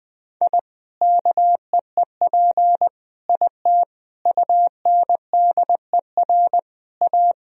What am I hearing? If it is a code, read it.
Morse code, heard as a keyed tone, about 21 wpm, 700 Hz. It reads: I KEEP IT UNDER A
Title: Sentences from Top 1000 Words - Review Part 2 - 20wpm